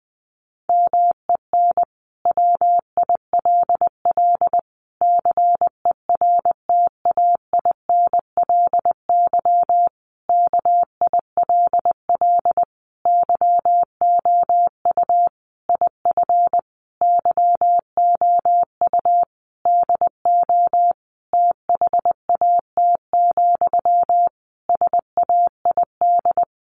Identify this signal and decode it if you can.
Morse code, heard as a keyed tone, about 20 wpm, 700 Hz. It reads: MEN WILL CERTAINLY KILL YOU IF YOU DO THAT, SAID
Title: Sentences from Top 1500 Words 20wpm